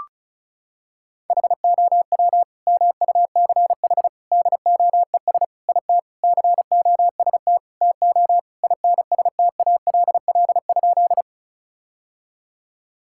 Morse code, heard as a keyed tone, about 36 wpm, 700 Hz. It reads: HOW MUCH DOES IT COST TO INSTALL?
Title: Sentences from Top 1600 Words 35wpm